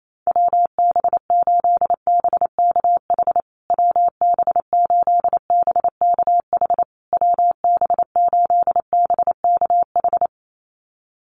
Morse code, heard as a keyed tone, about 28 wpm, 700 Hz. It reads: WB8BKH WB8BKH WB8BKH
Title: USA Call Signs Repeated 3x 28wpm